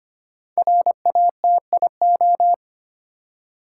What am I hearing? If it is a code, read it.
Morse code, heard as a keyed tone, about 26 wpm, 700 Hz. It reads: RATIO
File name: 5 Letter Words - First 3 Letters Given - 25wpm